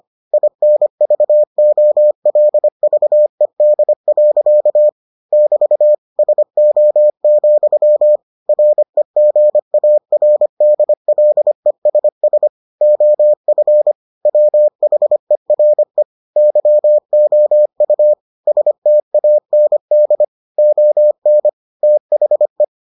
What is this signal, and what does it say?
Morse code, heard as a keyed tone, about 25 wpm, 600 Hz. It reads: INVOLVED. <BT> SO, REGARDLESS OF WHERE YOU STAND ON THE